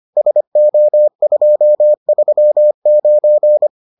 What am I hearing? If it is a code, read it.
Morse code, heard as a keyed tone, about 25 wpm, 600 Hz. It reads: SO239